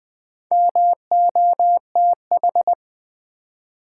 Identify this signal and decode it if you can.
Morse code, heard as a keyed tone, about 21 wpm, 700 Hz. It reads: MOTH
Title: Build A Word - 2 Letters At A Time - 6-Letter Words - 20wpm